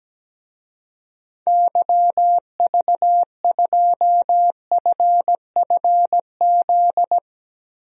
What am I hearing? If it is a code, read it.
Morse code, heard as a keyed tone, about 17 wpm, 700 Hz. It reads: YV2FFZ
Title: DXCC-100 Call Signs - 6 Characters - Repeated 3x - 17wpm